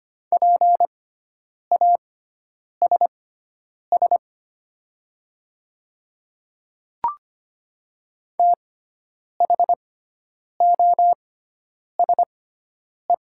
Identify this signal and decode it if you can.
Morse code, heard as a keyed tone, about 25 wpm, spaced as for 9 wpm, 700 Hz. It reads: PASS THOSE
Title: 6x Character Spacing - Top 500 Words - 25wpm